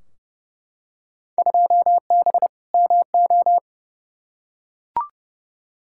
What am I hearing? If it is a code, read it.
Morse code, heard as a keyed tone, about 32 wpm, 700 Hz. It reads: 2B MO
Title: Field Day Exchanges - 30wpm